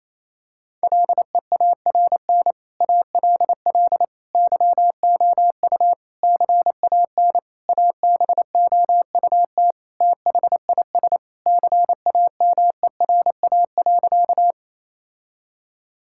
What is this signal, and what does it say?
Morse code, heard as a keyed tone, about 28 wpm, 700 Hz. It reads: LEARN ALL YOU CAN ABOUT THIS CAMERA.
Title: Sentences from Top 1000 Words - Review Part 5 - 28wpm